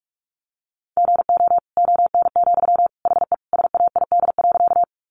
Morse code, heard as a keyed tone, about 44 wpm, 700 Hz. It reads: GO ON, HE SAID.
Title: Sentences from Top 100 Words 45wpm